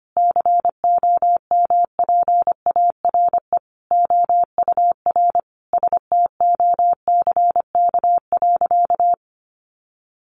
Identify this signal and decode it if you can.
Morse code, heard as a keyed tone, about 25 wpm, 700 Hz. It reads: COMPARE OUR STOCK.
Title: Sentences from Top 1300 Words 25wpm